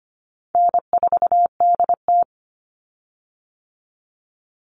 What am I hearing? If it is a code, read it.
Morse code, heard as a keyed tone, about 25 wpm, 700 Hz. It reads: N4DT
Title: Call Signs Repeated 2x - 25wpm